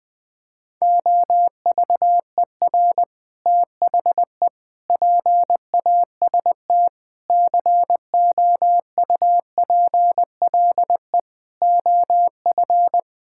Morse code, heard as a keyed tone, about 20 wpm, 700 Hz. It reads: OVER THE PAST COUPLE OF
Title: Sets of 6 Words - 20wpm